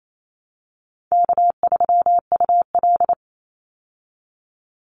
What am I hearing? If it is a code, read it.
Morse code, heard as a keyed tone, about 28 wpm, 700 Hz. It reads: K3UL
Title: Easy Call Signs Repeated 3x - 28wpm